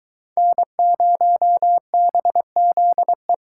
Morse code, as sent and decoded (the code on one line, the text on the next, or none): -. ----- -... --.. .
N0BZE